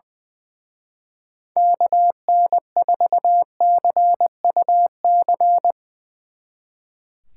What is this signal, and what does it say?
Morse code, heard as a keyed tone, about 22 wpm, 700 Hz. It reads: KN4CUC
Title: Speed Racing - USA 2x3 Call Signs 20wpm